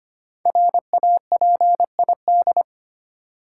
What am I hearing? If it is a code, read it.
Morse code, heard as a keyed tone, about 25 wpm, 700 Hz. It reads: RAPID
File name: 5 Letter Words - First 3 Letters Given - 25wpm